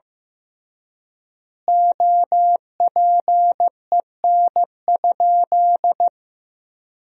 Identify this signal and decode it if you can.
Morse code, heard as a keyed tone, about 16 wpm, 700 Hz. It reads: OPEN?
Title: 3x Word-Spacing - Sentences from Top 300 Words - 15wpm